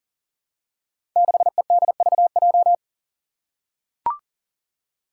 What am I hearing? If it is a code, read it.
Morse code, heard as a keyed tone, about 41 wpm, 700 Hz. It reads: 6EDVJ
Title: Code Groups - 5 Letter:Numbers - 40wpm